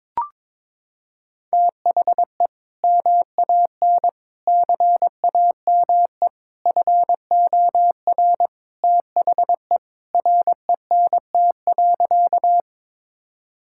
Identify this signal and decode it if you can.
Morse code, heard as a keyed tone, about 22 wpm, 700 Hz. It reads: THE MAN CAME FOR THE RENT.
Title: Sentences from Top 1000 Words - Review Part 5 - 22wpm